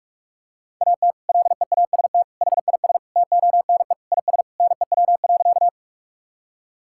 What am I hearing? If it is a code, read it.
Morse code, heard as a keyed tone, about 46 wpm, 700 Hz. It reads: AT LEAST HIS TONE IS NEW.
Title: Sentences from Top 1600 Words 45wpm